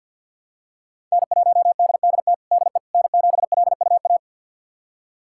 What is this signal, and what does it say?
Morse code, heard as a keyed tone, about 51 wpm, 700 Hz. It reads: N1DDT DE N7LUA
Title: USA Callsign DE Callsign - 5 Characters - 50wpm